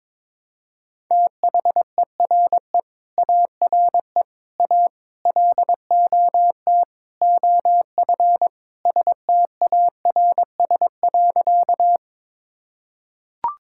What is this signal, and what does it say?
Morse code, heard as a keyed tone, about 22 wpm, 700 Hz. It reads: THERE ARE A LOT OF STARS.
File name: practice / 22wpm / Sentences with 4-Word Phrases - 22wpm